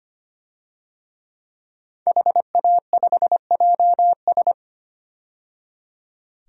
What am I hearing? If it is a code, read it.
Morse code, heard as a keyed tone, about 25 wpm, 700 Hz. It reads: HA5JS
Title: DXCC-100 Call Signs - 5 Characters - 25wpm